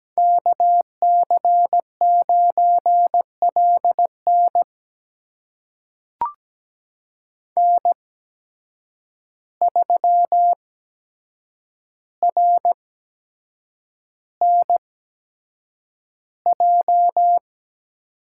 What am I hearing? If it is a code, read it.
Morse code, heard as a keyed tone, about 17 wpm, spaced as for 7 wpm, 700 Hz. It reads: KC9LN N3RNJ
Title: USA Call Signs - 5 Characters - 8x Character Spacing - 17wpm